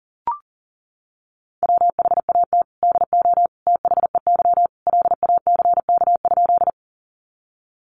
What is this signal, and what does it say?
Morse code, heard as a keyed tone, about 40 wpm, 700 Hz. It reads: WHAT DO THEY LACK?
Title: Sentences from Top 1100 Words 40wpm